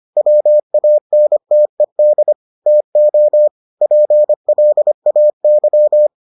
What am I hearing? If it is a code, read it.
Morse code, heard as a keyed tone, about 25 wpm, 600 Hz. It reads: WANTED TO PLAY